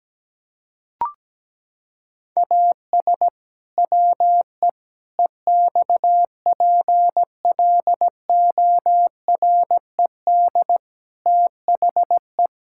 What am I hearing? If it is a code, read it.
Morse code, heard as a keyed tone, about 17 wpm, 700 Hz. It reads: AS WE EXPLORED THE
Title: Pangram Challenge v2 - 17wpm